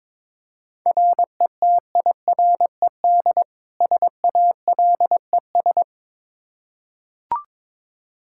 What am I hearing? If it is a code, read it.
Morse code, heard as a keyed tone, about 22 wpm, 700 Hz. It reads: RETIRED SALES
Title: Practice QSO - Exchange 3 - More Info - 22wpm